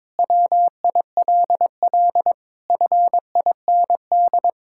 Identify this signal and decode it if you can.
Morse code, heard as a keyed tone, about 22 wpm, 700 Hz. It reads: WILL FIND
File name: practice / 22wpm / Sentences with 3-Word Phrases - 22wpm